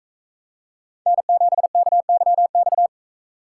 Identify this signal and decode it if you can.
Morse code, heard as a keyed tone, about 42 wpm, 700 Hz. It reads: N7KYX